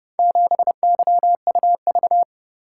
Morse code, heard as a keyed tone, about 30 wpm, 700 Hz. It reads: 7YUV